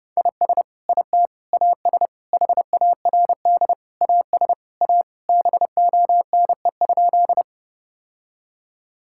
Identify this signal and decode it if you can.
Morse code, heard as a keyed tone, about 30 wpm, 700 Hz. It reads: IS IT AS HARD AS A BONE?
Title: Sentences from Top 1600 Words 30wpm